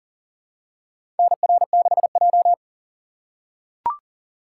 Morse code, heard as a keyed tone, about 40 wpm, 700 Hz. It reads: NR6J
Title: USA Call Signs - 4 Characters - 2x Character Spacing - 40wpm